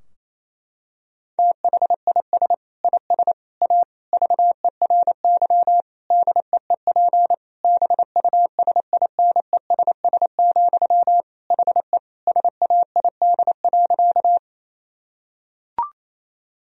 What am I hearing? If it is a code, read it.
Morse code, heard as a keyed tone, about 27 wpm, 700 Hz. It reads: THIS IS A VERY DEEP BUSINESS, HE SAID.
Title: Sentences from Top 1000 Words - Review Part 3 - 28wpm